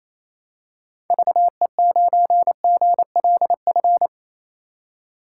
Morse code, heard as a keyed tone, about 28 wpm, 700 Hz. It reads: VE9GLF
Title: K1USN SST QSO Practice - 28wpm